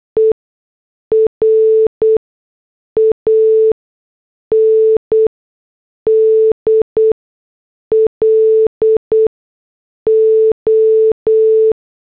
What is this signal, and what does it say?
Morse code, heard as a keyed tone, about 8 wpm, spaced as for 6 wpm, 440 Hz. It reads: ER AND LO